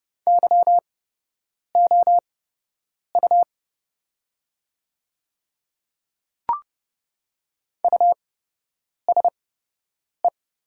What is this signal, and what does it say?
Morse code, heard as a keyed tone, about 29 wpm, spaced as for 8 wpm, 700 Hz. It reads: YOU USE
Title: 8x Character Spacing - Top 100 Words - 30wpm